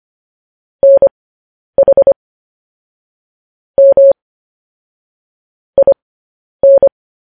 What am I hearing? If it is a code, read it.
Morse code, heard as a keyed tone, about 25 wpm, spaced as for 10 wpm, 550 Hz. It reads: NH M IN
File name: podcast / News headlines-1x-25x10